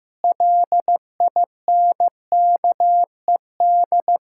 Morse code, heard as a keyed tone, about 15 wpm, 700 Hz. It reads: LINKED
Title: Pangram Challenge v2 - 15wpm